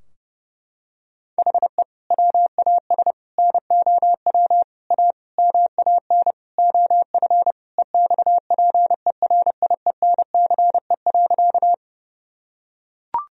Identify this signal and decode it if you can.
Morse code, heard as a keyed tone, about 30 wpm, 700 Hz. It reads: HE WAS NOW A MAN OF EXPERIENCE.
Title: Sentences from Top 1000 Words - Review Part 3 - 30wpm